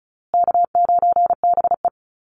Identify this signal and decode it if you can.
Morse code, heard as a keyed tone, about 34 wpm, 700 Hz. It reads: K9BE
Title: USA Call Signs Repeated 2x 35wpm